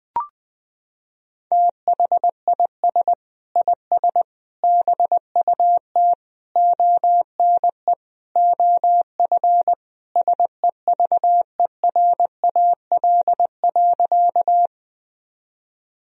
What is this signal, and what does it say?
Morse code, heard as a keyed tone, about 19 wpm, 700 Hz. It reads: THIS IS BUT ONE OF SEVERAL.
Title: Sentences from Top 1000 Words - Review Part 2 - 20wpm